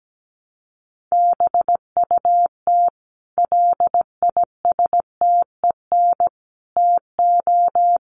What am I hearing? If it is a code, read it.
Morse code, heard as a keyed tone, about 17 wpm, 700 Hz. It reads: BUT LISTEN TO
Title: Sentences from Top 400 Words 17wpm